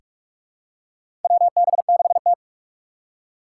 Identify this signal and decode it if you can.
Morse code, heard as a keyed tone, about 44 wpm, 700 Hz. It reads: WB6T
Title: USA Call Signs - 3x Character Spacing - 45wpm